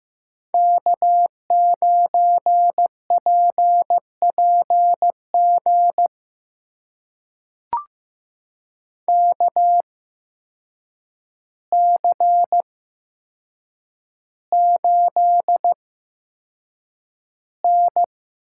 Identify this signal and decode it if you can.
Morse code, heard as a keyed tone, about 15 wpm, spaced as for 7 wpm, 700 Hz. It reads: K9PPG KC8N
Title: USA Call Signs - 8x Character Spacing - 15wpm